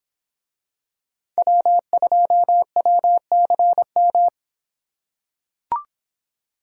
Morse code, heard as a keyed tone, about 24 wpm, 700 Hz. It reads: W2WCM